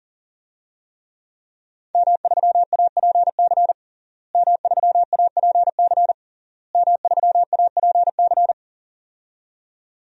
Morse code, heard as a keyed tone, about 40 wpm, 700 Hz. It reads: M3APC M3APC M3APC
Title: DXCC-100 Call Signs - 5 Characters - Repeated 3x - 40wpm